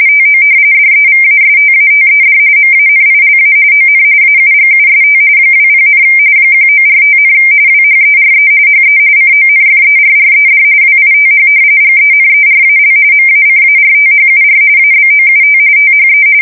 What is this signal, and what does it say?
RTTY, about 45.45 baud. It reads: TO DEVASTATED MANILA ON AUGUST 20, 1945, WHERE HE CONFERRED WITH GENERAL MACARTHUR ABOUT THE
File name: fsk170 / battleshipnj